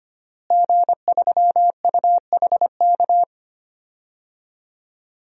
Morse code, heard as a keyed tone, about 25 wpm, 700 Hz. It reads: G3UHK
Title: DXCC-100 Call Signs - 5 Characters - Repeated 3x - 25wpm